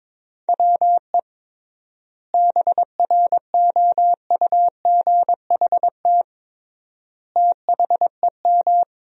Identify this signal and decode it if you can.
Morse code, heard as a keyed tone, about 22 wpm, 700 Hz. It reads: WE BROUGHT THEM
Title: 3x Word-Spacing - Sentences from Top 500 Words - 22wpm